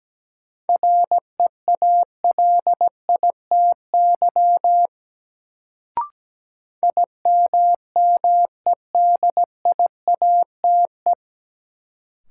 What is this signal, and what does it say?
Morse code, heard as a keyed tone, about 17 wpm, 700 Hz. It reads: REALITY IMMEDIATE
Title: Sentences from Top 1200 Words - New Word Drill - 17wpm